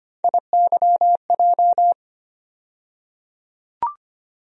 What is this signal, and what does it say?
Morse code, heard as a keyed tone, about 25 wpm, 700 Hz. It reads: IYJ
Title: USA Call Sign Postfixes - 3 Letters 25wpm